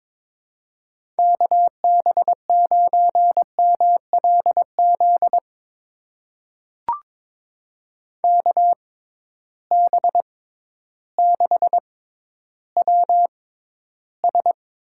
Morse code, heard as a keyed tone, about 22 wpm, spaced as for 12 wpm, 700 Hz. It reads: KB9MLZ KB6WS
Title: USA Call Signs - 6 Characters - 6x Character Spacing - 22wpm